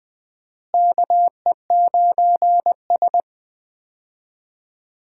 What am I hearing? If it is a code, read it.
Morse code, heard as a keyed tone, about 20 wpm, 700 Hz. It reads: KE9S
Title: USA Call Signs - 4 Characters - Repeated 2x - 20wpm